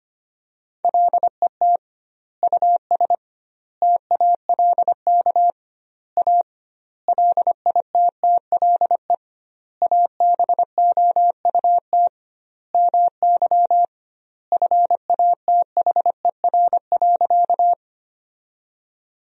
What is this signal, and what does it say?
Morse code, heard as a keyed tone, about 25 wpm, 700 Hz. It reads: LET US TALK A LITTLE ABOUT MY FATHER.
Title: 2x Word-Spacing - Sentences from Top 400 Words - 25wpm